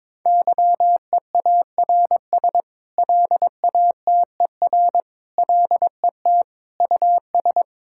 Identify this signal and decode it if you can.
Morse code, heard as a keyed tone, about 22 wpm, 700 Hz. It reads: YEARS LATER LET US
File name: practice / 22wpm / Sentences with 8-Word Phrases - 22wpm